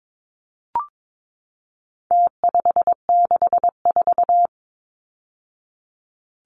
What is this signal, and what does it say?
Morse code, heard as a keyed tone, about 22 wpm, 700 Hz. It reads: T564
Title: Numbers - Contest Style - 4-Digit 22wpm